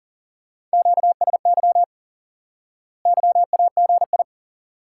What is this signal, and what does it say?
Morse code, heard as a keyed tone, about 37 wpm, 700 Hz. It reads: QSY YAGI
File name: course / instant-qso-element-course-3 / IQE Course 3 - Lesson 89 - YAGI - Two Elements - 40wpm